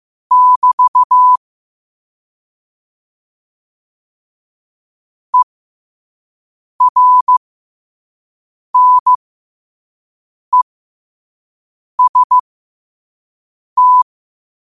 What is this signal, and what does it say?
Morse code, heard as a keyed tone, about 15 wpm, spaced as for 5 wpm, 1000 Hz. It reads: <BT> ERNEST
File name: qotdfiles / 5wpm